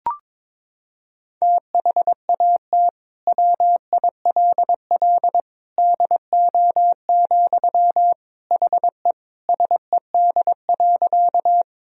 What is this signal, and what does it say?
Morse code, heard as a keyed tone, about 21 wpm, 700 Hz. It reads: THAT WILL DO, HE SED.
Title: Sentences from Top 100 Words and Abbreviations 22wpm